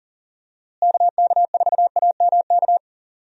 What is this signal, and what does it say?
Morse code, heard as a keyed tone, about 40 wpm, 700 Hz. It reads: KK4AMK
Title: USA Call Signs - 6 Characters - Repeated 3x - 40wpm